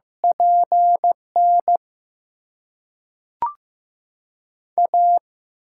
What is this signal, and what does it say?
Morse code, heard as a keyed tone, about 15 wpm, 700 Hz. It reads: PN A
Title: USA Call Signs - 5 Characters - 6x Character Spacing - 15wpm